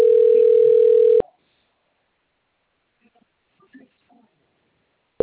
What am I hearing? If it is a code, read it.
Ringback tone (North America), 440 + 480 Hz.